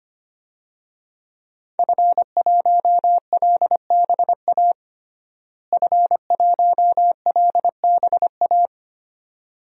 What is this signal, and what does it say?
Morse code, heard as a keyed tone, about 25 wpm, 700 Hz. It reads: F1LBA F1LBA
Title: DXCC-100 Call Signs - 5 Characters - Repeated 2x - 25wpm